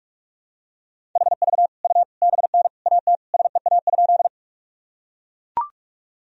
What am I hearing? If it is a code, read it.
Morse code, heard as a keyed tone, about 43 wpm, 700 Hz. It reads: HV U BN AT SEA?
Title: Sentences from Top 300 Words and Abbreviations 45wpm